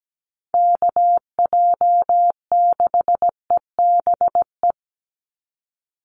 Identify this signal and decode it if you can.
Morse code, heard as a keyed tone, about 17 wpm, 700 Hz. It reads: KJ6EBE
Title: USA Call Signs 17wpm